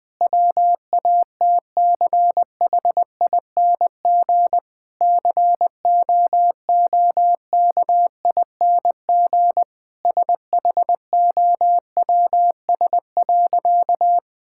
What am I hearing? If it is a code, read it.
Morse code, heard as a keyed tone, about 20 wpm, 700 Hz. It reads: WATCHING COOKING SHOWS.
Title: Difficult Sentences with Hints - 20wpm